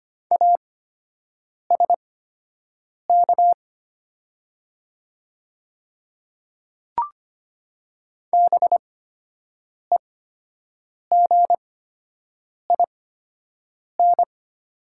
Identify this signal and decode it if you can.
Morse code, heard as a keyed tone, about 25 wpm, spaced as for 7 wpm, 700 Hz. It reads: ASK BEGIN
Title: 8x Character Spacing - Top 300 Words - 25wpm